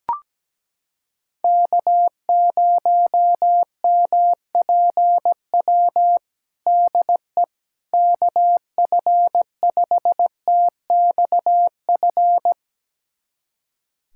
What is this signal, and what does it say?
Morse code, heard as a keyed tone, about 17 wpm, 700 Hz. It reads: K0MPW DE KF5TXF
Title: USA Callsign DE Callsign 17wpm